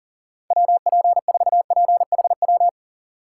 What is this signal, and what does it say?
Morse code, heard as a keyed tone, about 40 wpm, 700 Hz. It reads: WP4PHW
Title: USA Call Signs - 6 Characters - 6x Character Spacing - 40wpm